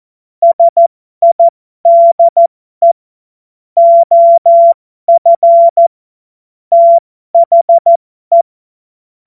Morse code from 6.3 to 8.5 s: THE